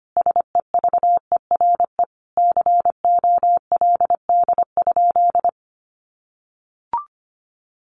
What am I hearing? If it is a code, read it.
Morse code, heard as a keyed tone, about 25 wpm, 700 Hz. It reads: SEVERE COLD?
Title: Sentences from Top 1400 Words 25wpm